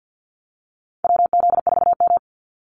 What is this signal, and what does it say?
Morse code, heard as a keyed tone, about 50 wpm, 700 Hz. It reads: WZ4M
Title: NA SOTA Activator Callsigns 50wpm